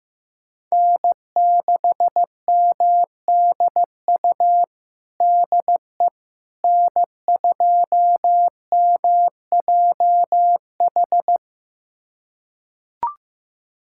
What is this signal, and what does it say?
Morse code, heard as a keyed tone, about 15 wpm, 700 Hz. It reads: N6MDU DE N2MJH
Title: USA Callsign DE Callsign - 5 Characters - 15wpm